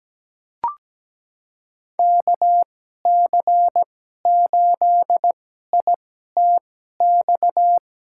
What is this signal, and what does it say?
Morse code, heard as a keyed tone, about 17 wpm, spaced as for 12 wpm, 700 Hz. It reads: KC8ITX
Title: USA Call Signs - 2x Character Spacing - 17wpm